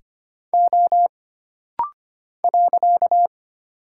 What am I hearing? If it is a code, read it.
Morse code, heard as a keyed tone, about 25 wpm, 700 Hz. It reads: O .
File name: Single Character (Letter:Number:Punctuation:Prosign) - Rapid-Fire - 25wpm